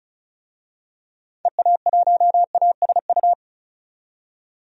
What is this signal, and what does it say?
Morse code, heard as a keyed tone, about 36 wpm, 700 Hz. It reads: EA1ASU
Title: DXCC-100 Call Signs - 6 Characters - 35wpm